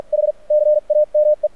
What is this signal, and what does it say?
Morse code, heard as a keyed tone, about 48 wpm, spaced as for 37 wpm, 600 Hz. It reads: HCAME
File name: archived / CRSnet